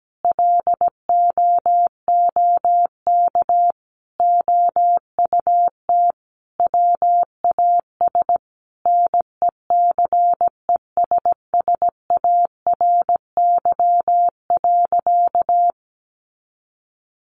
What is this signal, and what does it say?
Morse code, heard as a keyed tone, about 17 wpm, 700 Hz. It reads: LOOK OUT WAS NECESSARY.
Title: Sentences from Top 1500 Words 17wpm